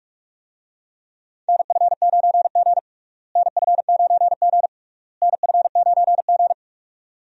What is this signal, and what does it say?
Morse code, heard as a keyed tone, about 42 wpm, 700 Hz. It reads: NF9G NF9G NF9G
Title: USA Call Signs - 4 Characters - Repeated 3x - 45wpm